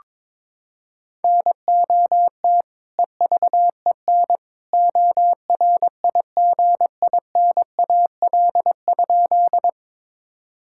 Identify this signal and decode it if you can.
Morse code, heard as a keyed tone, about 22 wpm, 700 Hz. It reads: NOT EVEN ORIGINAL?
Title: Sentences from Top 1000 Words - Review Part 6 - 22wpm